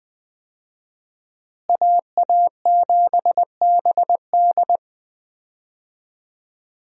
Morse code, heard as a keyed tone, about 20 wpm, 700 Hz. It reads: AA7BD
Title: USA Call Signs - 5 Characters - Repeated 2x - 20wpm